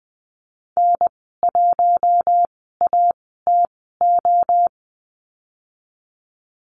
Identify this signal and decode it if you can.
Morse code, heard as a keyed tone, about 20 wpm, spaced as for 14 wpm, 700 Hz. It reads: N1ATO
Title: 076 Lesson 13g - Letter L - Call Signs 20wpm